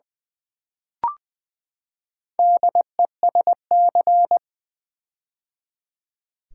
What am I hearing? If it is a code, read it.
Morse code, heard as a keyed tone, about 20 wpm, 700 Hz. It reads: DESC
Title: Build A Word - 4 Letters At A Time - 11-Letter Words - 20wpm